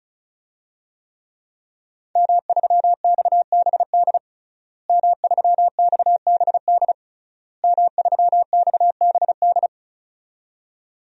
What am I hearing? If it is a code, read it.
Morse code, heard as a keyed tone, about 36 wpm, 700 Hz. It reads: M3XBD M3XBD M3XBD
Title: DXCC-100 Call Signs - 5 Characters - Repeated 3x - 35wpm